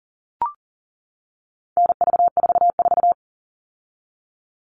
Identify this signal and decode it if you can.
Morse code, heard as a keyed tone, about 40 wpm, 700 Hz. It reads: NV44
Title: All Cut Numbers - 4-Digit 40wpm